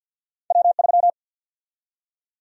Morse code, heard as a keyed tone, about 51 wpm, 700 Hz. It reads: W3
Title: USA Call Sign Prefixes - 50wpm